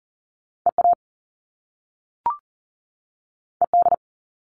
Morse code, heard as a keyed tone, about 41 wpm, 700 Hz. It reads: EA ED
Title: All Cut Numbers - 2-Digit 40wpm